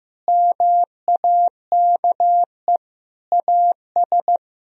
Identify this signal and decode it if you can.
Morse code, heard as a keyed tone, about 15 wpm, 700 Hz. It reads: MAKE AS
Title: Sentences from Top 1000 Words - Review Part 3 - 15wpm